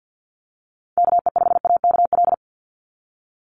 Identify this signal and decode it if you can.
Morse code, heard as a keyed tone, about 50 wpm, 700 Hz. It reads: KE5AKL
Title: NA SOTA Activator Callsigns 50wpm